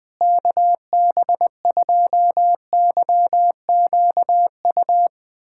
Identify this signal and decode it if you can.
Morse code, heard as a keyed tone, about 21 wpm, 690 Hz. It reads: KB2YQU
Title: USA Callsign DE Callsign 20wpm